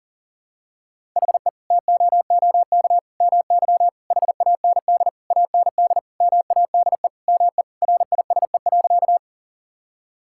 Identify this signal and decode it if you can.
Morse code, heard as a keyed tone, about 37 wpm, 700 Hz. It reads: HE TOOK MY HAND AND MADE ME RISE.
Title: Sentences from Top 1000 Words - Review Part 6 - 40wpm